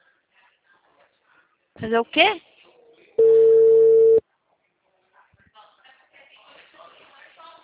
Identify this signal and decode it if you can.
Ringback tone (China, 450 Hz), with speech.